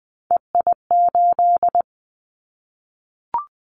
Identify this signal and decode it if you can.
Morse code, heard as a keyed tone, about 20 wpm, 700 Hz. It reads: EI8
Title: Callsign Prefixes for Top 50 DXCC - 20wpm